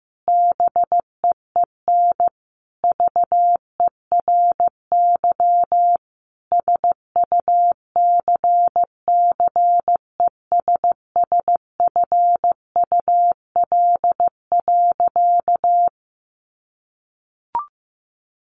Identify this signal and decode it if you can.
Morse code, heard as a keyed tone, about 15 wpm, 700 Hz. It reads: BEEN VERY SUCCESSFUL.
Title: Sentences from Top 1300 Words 15wpm